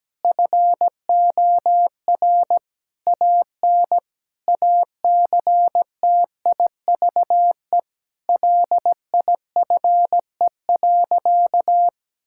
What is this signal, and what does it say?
Morse code, heard as a keyed tone, about 17 wpm, 700 Hz. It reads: FOR AN ACTIVE LIFE.
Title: Sentences from Top 1000 Words - Review Part 6 - 17wpm